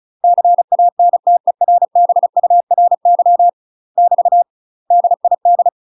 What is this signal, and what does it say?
Morse code, heard as a keyed tone, about 36 wpm, 700 Hz. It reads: CANTERBURY <BT> DID
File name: UK MASTER QSOs TRAINING / 35WPM / UK MASTER QSOs 0009